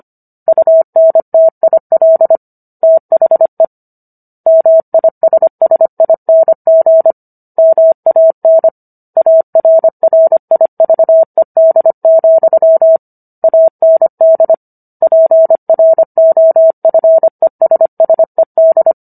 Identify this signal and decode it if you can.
Morse code, heard as a keyed tone, about 25 wpm, 650 Hz. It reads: UNTIL THE MISSING MAN ARRIVED, AND PROFESSED